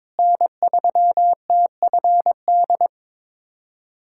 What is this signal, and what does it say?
Morse code, heard as a keyed tone, about 22 wpm, 700 Hz. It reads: N3TFD
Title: USA Callsign DE Callsign - 5 Characters - 22wpm